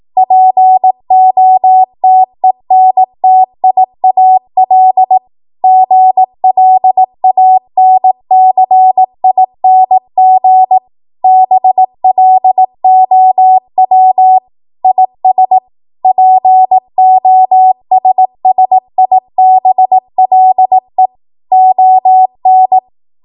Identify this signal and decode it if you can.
Morse code, heard as a keyed tone, about 18 wpm, 750 Hz. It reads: POTENTIAL GLANCING BLOW IS POSSIBLE ON
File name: Morse / 18WPM